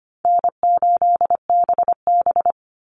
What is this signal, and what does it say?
Morse code, heard as a keyed tone, about 25 wpm, 700 Hz. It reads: N8BB
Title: NA SOTA Chaser Callsigns 25wpm